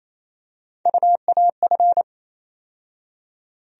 Morse code, heard as a keyed tone, about 27 wpm, 700 Hz. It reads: UAF
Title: USA Call Sign Postfixes - 3 Letters 28wpm